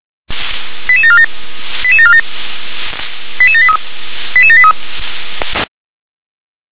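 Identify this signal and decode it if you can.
Five-tone selective call (ZVEI), heard as quick runs of five tones, tones of about 70 ms. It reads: 88747 88747 88783 88783
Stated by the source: ZVEI five-tone selective call received on SDR, address 88080.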